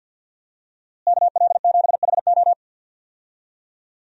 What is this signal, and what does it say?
Morse code, heard as a keyed tone, about 50 wpm, 700 Hz. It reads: KL7HO
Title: DXCC-100 Call Signs - 5 Characters - Repeated 2x - 50wpm